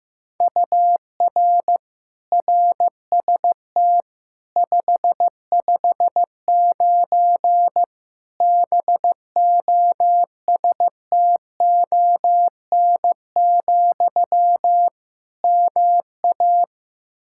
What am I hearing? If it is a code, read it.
Morse code, heard as a keyed tone, about 15 wpm, 700 Hz. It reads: UR RST 559 BOSTON, MA
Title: K1USN SST QSO Practice 15wpm